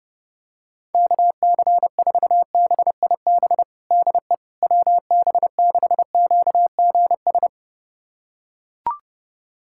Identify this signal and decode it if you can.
Morse code, heard as a keyed tone, about 30 wpm, 700 Hz. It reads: KC4BIB DE WB6QGS
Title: USA Callsign DE Callsign - 6 Characters - 30wpm